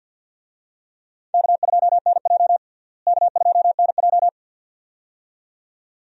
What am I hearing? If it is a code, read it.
Morse code, heard as a keyed tone, about 47 wpm, 700 Hz. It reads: K2NJ K2NJ
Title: USA Call Signs - 4 Characters - Repeated 2x - 50wpm